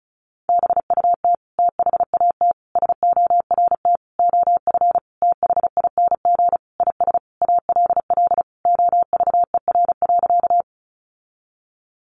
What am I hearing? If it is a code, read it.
Morse code, heard as a keyed tone, about 34 wpm, 700 Hz. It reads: BUT THAT SORT OF THING IS ALL OVER.
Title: Sentences from Top 1300 Words 35wpm